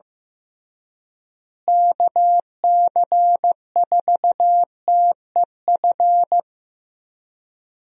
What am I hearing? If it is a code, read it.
Morse code, heard as a keyed tone, about 15 wpm, 700 Hz. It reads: KC4TEF
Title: USA Call Signs - 6 Characters - Repeated 2x - 15wpm